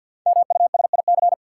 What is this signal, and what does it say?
Morse code, heard as a keyed tone, about 49 wpm, 700 Hz. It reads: MUSIC
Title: Sentences from Top 300 Words - New Word Drill - 50wpm